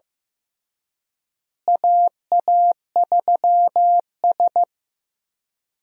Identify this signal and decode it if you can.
Morse code, heard as a keyed tone, about 15 wpm, 700 Hz. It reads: AA3S
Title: USA Call Signs - 4 Characters - Repeated 2x - 15wpm